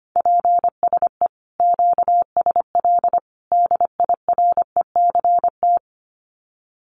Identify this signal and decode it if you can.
Morse code, heard as a keyed tone, about 25 wpm, 700 Hz. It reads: PSE QSL DIRECT
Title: Common QSO Phrases 25wpm